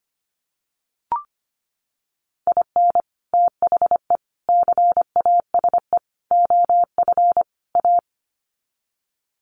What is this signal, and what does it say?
Morse code, heard as a keyed tone, about 25 wpm, 700 Hz. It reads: IN THE CASE OF A
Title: Sets of 5 Words Encore 25wpm